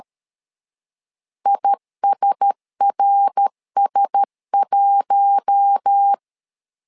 Morse code, heard as a keyed tone, about 13 wpm, 800 Hz. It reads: ISRS1
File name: LEZIONI / 4 Q1S 12wpm 1802